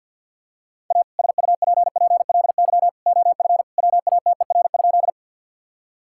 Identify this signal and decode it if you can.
Morse code, heard as a keyed tone, about 50 wpm, 700 Hz. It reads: A SUPPLY OF WATER?
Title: Sentences from Top 1000 Words - Review Part 5 - 50wpm